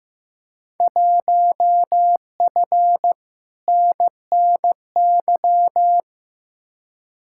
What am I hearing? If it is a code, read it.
Morse code, heard as a keyed tone, about 15 wpm, 700 Hz. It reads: 1F NNY
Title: Field Day Exchanges - 15wpm